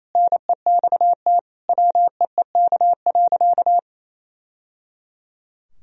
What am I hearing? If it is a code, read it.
Morse code, heard as a keyed tone, about 27 wpm, 700 Hz. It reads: NEXT WEEK.
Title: Sentences from Top 1000 Words - Review Part 2 - 28wpm